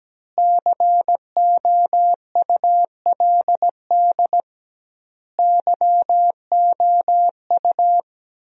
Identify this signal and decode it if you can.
Morse code, heard as a keyed tone, about 17 wpm, 700 Hz. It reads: COULD YOU
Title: 2x Word-Spacing - Sentences from Top 300 Words - 17wpm